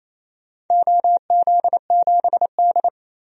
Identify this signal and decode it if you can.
Morse code, heard as a keyed tone, about 28 wpm, 700 Hz. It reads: OZ7D
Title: DXCC-100 Call Signs - 4 Characters - 28wpm